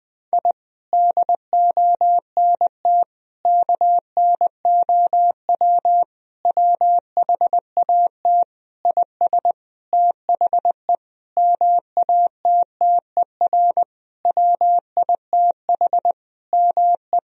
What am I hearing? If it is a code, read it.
Morse code, heard as a keyed tone, about 20 wpm, 700 Hz. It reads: I DONT KNOW WHAT IS THE MATTER WITH ME
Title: Sentences from Top 1000 Words - Review Part 4 - 20wpm